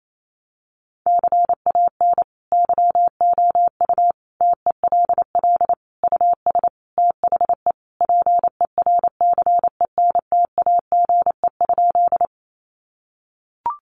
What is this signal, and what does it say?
Morse code, heard as a keyed tone, about 28 wpm, 700 Hz. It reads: CAN YOU TELL US THE PERCENTAGE?
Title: Sentences from Top 1300 Words 28wpm